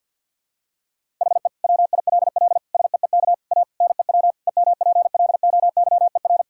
Morse code, heard as a keyed tone, about 51 wpm, 700 Hz. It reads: HE WILL SEEK A NEW EMPLOYER
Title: Sentences from Top 1600 Words 50wpm